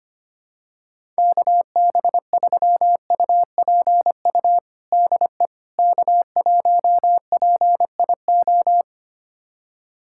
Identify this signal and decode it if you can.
Morse code, heard as a keyed tone, about 25 wpm, 700 Hz. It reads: KB3UPU DE K1PIO
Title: USA Callsign DE Callsign 25wpm